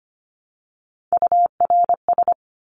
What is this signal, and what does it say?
Morse code, heard as a keyed tone, about 25 wpm, 700 Hz. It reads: URS
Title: Abbreviations Repeated 3x - 25wpm